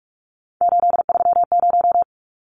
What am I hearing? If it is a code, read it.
Morse code, heard as a keyed tone, about 44 wpm, 700 Hz. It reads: 830
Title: Numbers - 3-Digit 45wpm